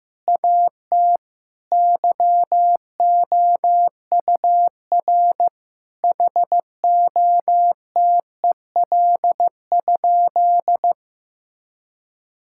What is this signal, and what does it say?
Morse code, heard as a keyed tone, about 15 wpm, 700 Hz. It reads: AT YOUR HOTEL?
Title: Sentences from Top 1600 Words 15wpm